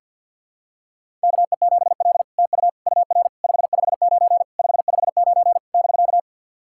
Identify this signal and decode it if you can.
Morse code, heard as a keyed tone, about 50 wpm, 700 Hz. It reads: KE7L TU UR 559 559 <BK>
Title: SOTA QSO Practice 50wpm